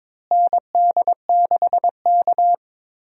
Morse code, heard as a keyed tone, about 22 wpm, 700 Hz. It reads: ND6K
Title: USA Call Signs - 4 Characters - Repeated 3x - 22wpm